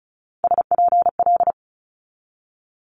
Morse code, heard as a keyed tone, about 36 wpm, 700 Hz. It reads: SPL
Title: Multi-letter Phonemes 35wpm